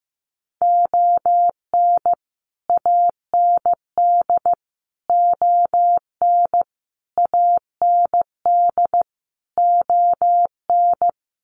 Morse code, heard as a keyed tone, about 15 wpm, 700 Hz. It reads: ON AND ON AND ON
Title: Sets of 5 Words Encore2 15wpm